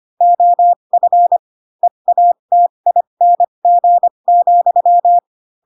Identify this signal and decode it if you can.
Morse code, heard as a keyed tone, about 25 wpm, spaced as for 22 wpm, 700 Hz. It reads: OF EATING,
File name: Pinocchio / Pinocchio ch4 18 25 ik1hge